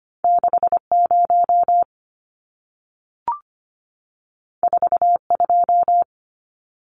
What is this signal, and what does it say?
Morse code, heard as a keyed tone, about 25 wpm, 700 Hz. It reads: 60 42
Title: Numbers - 2-Digit 25wpm